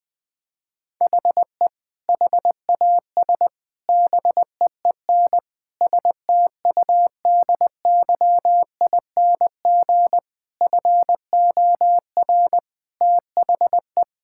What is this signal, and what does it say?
Morse code, heard as a keyed tone, about 20 wpm, 700 Hz. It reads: HE HAS BEEN STUDYING FOR THE
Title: Sentences with 7-Word Phrases - Spoken Beginning - 20wpm